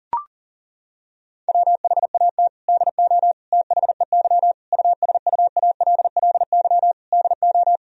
Morse code, heard as a keyed tone, about 40 wpm, 700 Hz. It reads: WHAT DO THEY USUALLY DO
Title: Sentences from Top 1000 Words - Review Part 3 - 40wpm